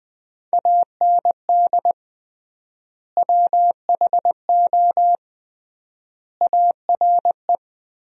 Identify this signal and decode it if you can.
Morse code, heard as a keyed tone, about 20 wpm, 700 Hz. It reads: AND WHO ARE
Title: 3x Word-Spacing - Sentences from Top 100 Words - 20wpm